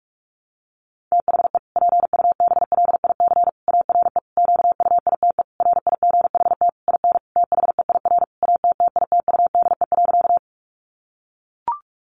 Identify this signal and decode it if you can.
Morse code, heard as a keyed tone, about 44 wpm, 700 Hz. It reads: THE PUBLIC ARE QUITE RIGHT IN THEIR ATTITUDE.
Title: Sentences from Top 1300 Words 45wpm